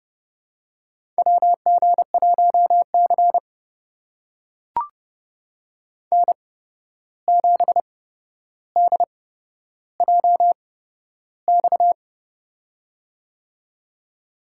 Morse code, heard as a keyed tone, about 30 wpm, spaced as for 12 wpm, 700 Hz. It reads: WG1C N7DJX
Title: USA Call Signs - 8x Character Spacing - 30wpm